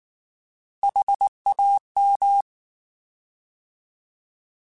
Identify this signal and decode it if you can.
Morse code, heard as a keyed tone, about 19 wpm, 800 Hz. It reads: HAM